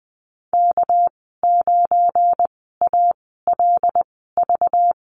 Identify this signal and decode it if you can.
Morse code, heard as a keyed tone, about 20 wpm, spaced as for 14 wpm, 700 Hz. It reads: K9ALV
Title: 231 Lesson 37g - Letter x - Call Signs 20wpm